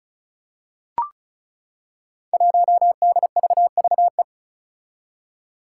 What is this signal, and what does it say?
Morse code, heard as a keyed tone, about 34 wpm, 700 Hz. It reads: 1DVVE
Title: Five Challenging Letter-Number Combinations 35wpm